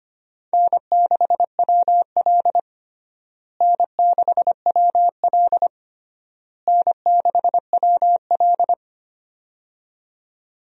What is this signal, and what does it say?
Morse code, heard as a keyed tone, about 25 wpm, 700 Hz. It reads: N6WL N6WL N6WL
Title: USA Call Signs - 4 Characters - Repeated 3x - 25wpm